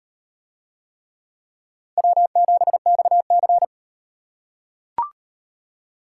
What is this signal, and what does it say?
Morse code, heard as a keyed tone, about 38 wpm, 700 Hz. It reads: W7XC